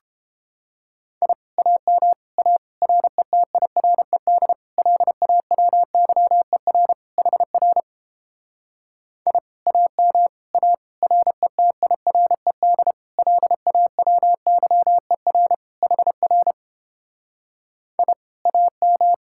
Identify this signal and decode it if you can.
Morse code, heard as a keyed tone, about 29 wpm, 700 Hz. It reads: I AM A RETIRED LAWYER HR I AM A RETIRED LAWYER HR I AM